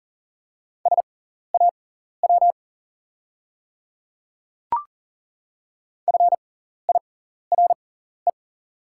Morse code, heard as a keyed tone, about 40 wpm, spaced as for 14 wpm, 690 Hz. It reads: SAW FIRE
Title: 6x Character Spacing - Top 500 Words - 40wpm